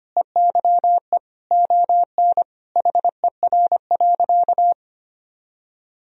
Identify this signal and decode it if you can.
Morse code, heard as a keyed tone, about 25 wpm, 700 Hz. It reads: EYE ON HER.
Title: Sentences from Top 1000 Words - Review Part 1 - 25wpm